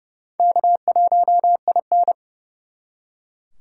Morse code, heard as a keyed tone, about 37 wpm, 700 Hz. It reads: K1IN
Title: Speed Racing - USA 1x2 Call Signs 30wpm